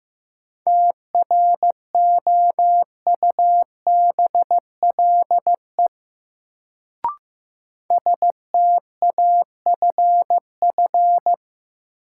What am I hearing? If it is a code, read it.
Morse code, heard as a keyed tone, about 15 wpm, 700 Hz. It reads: TROUBLE STAFF
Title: Sentences from Top 1200 Words - New Word Drill - 15wpm